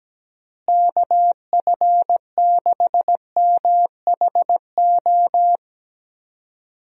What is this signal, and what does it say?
Morse code, heard as a keyed tone, about 17 wpm, 700 Hz. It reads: KF6MHO
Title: USA Call Signs - 6 Characters - Repeated 2x - 17wpm